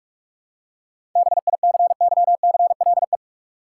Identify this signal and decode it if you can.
Morse code, heard as a keyed tone, about 46 wpm, 700 Hz. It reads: BICYCLE
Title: Seven Letter Words - 45wpm